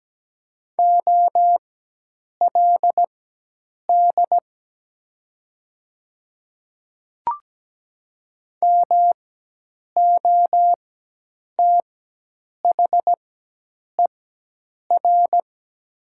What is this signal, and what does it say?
Morse code, heard as a keyed tone, about 17 wpm, spaced as for 8 wpm, 700 Hz. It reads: OLD MOTHER
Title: 4x Character Spacing - Top 200 Words - 17wpm